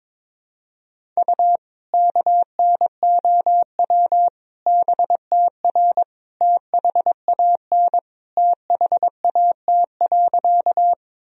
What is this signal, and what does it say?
Morse code, heard as a keyed tone, about 22 wpm, 700 Hz. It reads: U KNOW BTR THAN THAT.
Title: Sentences from Top 400 Words and Abbreviations 22wpm